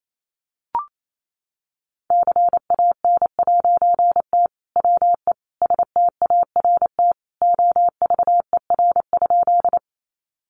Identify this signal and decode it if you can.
Morse code, heard as a keyed tone, about 28 wpm, 700 Hz. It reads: CAN'T WE START OVER?
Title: A Sound Of Thunder 28wpm